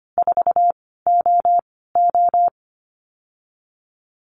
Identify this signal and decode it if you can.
Morse code, heard as a keyed tone, about 25 wpm, 700 Hz. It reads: 4 O O
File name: Lesson 15g - Number 5 - Call Signs 25wpm